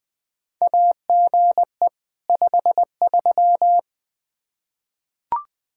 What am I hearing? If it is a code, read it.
Morse code, heard as a keyed tone, about 20 wpm, 700 Hz. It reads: AGE 53
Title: K1USN SST QSO Practice 20wpm